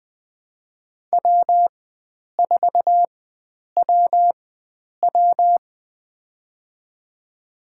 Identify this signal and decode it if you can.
Morse code, heard as a keyed tone, about 21 wpm, spaced as for 9 wpm, 700 Hz. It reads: W4WW
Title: USA Call Signs - 4 Characters - 4x Character Spacing - 20wpm